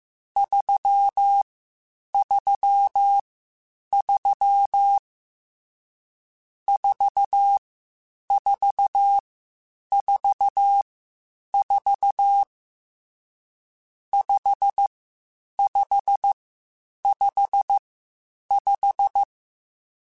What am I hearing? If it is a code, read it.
Morse code, heard as a keyed tone, about 15 wpm, 800 Hz. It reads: 3 3 3 4 4 4 4 5 5 5 5